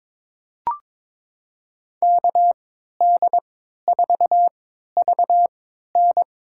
Morse code, heard as a keyed tone, about 22 wpm, spaced as for 12 wpm, 700 Hz. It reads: KD4VN
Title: USA Call Signs - 3x Character Spacing - 22wpm